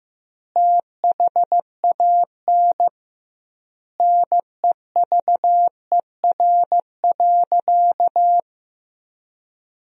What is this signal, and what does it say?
Morse code, heard as a keyed tone, about 15 wpm, 700 Hz. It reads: THAN NEVER.
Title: 2x Word-Spacing - Sentences from Top 400 Words - 15wpm